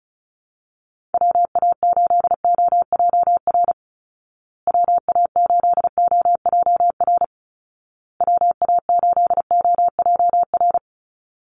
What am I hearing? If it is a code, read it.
Morse code, heard as a keyed tone, about 34 wpm, 700 Hz. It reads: WA8OJR WA8OJR WA8OJR
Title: Call Signs Repeated 2x - 35wpm